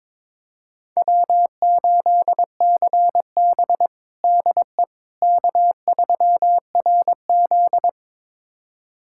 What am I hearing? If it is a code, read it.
Morse code, heard as a keyed tone, about 22 wpm, 700 Hz. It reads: W8CB DE K3RZ
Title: USA Callsign DE Callsign - 4 Characters - 22wpm